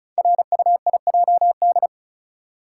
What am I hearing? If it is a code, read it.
Morse code, heard as a keyed tone, about 36 wpm, 700 Hz. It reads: RUIJD
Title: Code Groups - 5 Letters - 35wpm